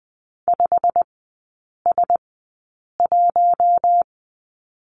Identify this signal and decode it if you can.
Morse code, heard as a keyed tone, about 20 wpm, spaced as for 8 wpm, 690 Hz. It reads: 5S1
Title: 093 Lesson 16e - Letter c - Three Characters 20wpm